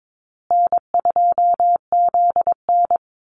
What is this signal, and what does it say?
Morse code, heard as a keyed tone, about 22 wpm, 700 Hz. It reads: N2ZN
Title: Call Signs Repeated 2x - 22wpm